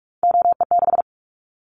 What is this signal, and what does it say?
Morse code, heard as a keyed tone, about 46 wpm, 700 Hz. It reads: OE6
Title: Callsign Prefixes for Top 50 DXCC - 45wpm